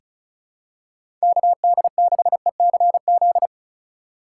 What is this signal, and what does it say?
Morse code, heard as a keyed tone, about 36 wpm, 700 Hz. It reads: KD6ECZ
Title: USA Call Signs - 6 Characters - 8x Character Spacing - 35wpm